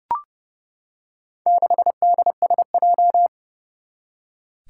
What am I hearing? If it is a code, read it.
Morse code, heard as a keyed tone, about 30 wpm, 700 Hz. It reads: 6DSJ
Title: Four Challenging Letter-Number Combinations 30wpm